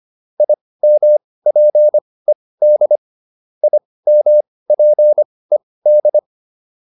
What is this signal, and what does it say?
Morse code, heard as a keyed tone, about 25 wpm, spaced as for 18 wpm, 600 Hz. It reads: IMPED IMPED
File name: IM5 18